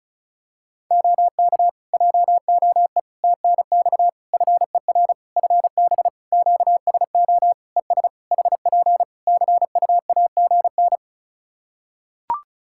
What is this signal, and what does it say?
Morse code, heard as a keyed tone, about 36 wpm, 700 Hz. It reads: OK JOE TNX FER FB QSO ES HP CUAGN
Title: Practice QSO - Exchange 4 - Ending - 35wpm